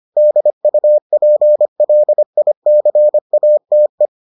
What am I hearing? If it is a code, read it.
Morse code, heard as a keyed tone, about 25 wpm, 600 Hz. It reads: DUPLICATE